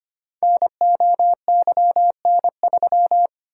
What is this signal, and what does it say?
Morse code, heard as a keyed tone, about 25 wpm, 700 Hz. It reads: NOYN3
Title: Code Groups - 5 Letter:Numbers - 25wpm